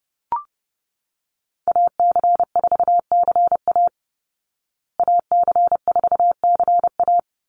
Morse code, heard as a keyed tone, about 28 wpm, 700 Hz. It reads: AC4CA AC4CA
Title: Call Signs Repeated 3x - 30wpm